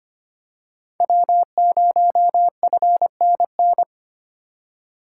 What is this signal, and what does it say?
Morse code, heard as a keyed tone, about 25 wpm, 700 Hz. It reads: W0FNN
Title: USA Call Signs - 5 Characters - 3x Character Spacing - 25wpm